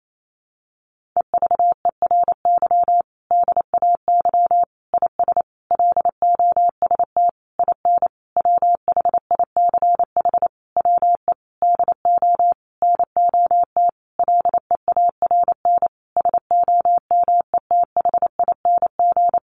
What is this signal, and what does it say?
Morse code, heard as a keyed tone, about 28 wpm, 700 Hz. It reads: EVERY DAY IS LOST IN WHICH WE DO NOT LEARN SOMETHING
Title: Sentences from Top 1400 Words 28wpm